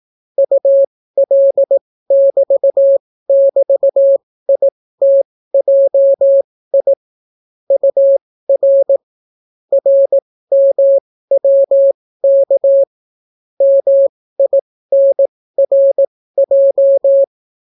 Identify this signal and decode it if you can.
Morse code, heard as a keyed tone, about 18 wpm, spaced as for 14 wpm, 550 Hz. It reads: UL<BT><BT>ITJI UR RMWK MINRJ